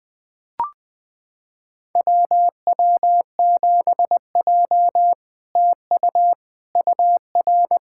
Morse code, heard as a keyed tone, about 21 wpm, 700 Hz. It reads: WW7J TU UR
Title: SOTA QSO Practice 20wpm